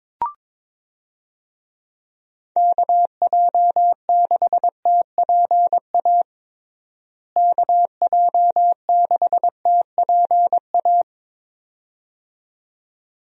Morse code, heard as a keyed tone, about 22 wpm, 700 Hz. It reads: KJ6TPA KJ6TPA
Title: USA Call Signs - 6 Characters - Repeated 2x - 22wpm